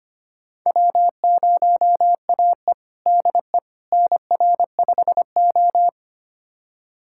Morse code, heard as a keyed tone, about 25 wpm, 700 Hz. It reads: W0AE DE NR5O
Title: USA Callsign DE Callsign - 4 Characters - 25wpm